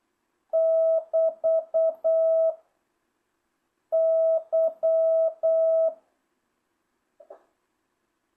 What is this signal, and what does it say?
Morse code, heard as a keyed tone, about 8 wpm, 650 Hz. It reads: <BT> Y